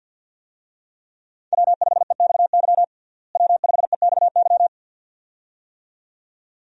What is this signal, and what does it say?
Morse code, heard as a keyed tone, about 47 wpm, 700 Hz. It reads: W5EXY W5EXY
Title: USA Call Signs - 5 Characters - Repeated 2x - 50wpm